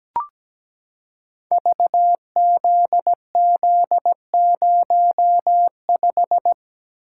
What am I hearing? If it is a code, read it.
Morse code, heard as a keyed tone, about 17 wpm, 700 Hz. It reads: VZZ05
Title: Code Groups - 5 Letter:Numbers - 17wpm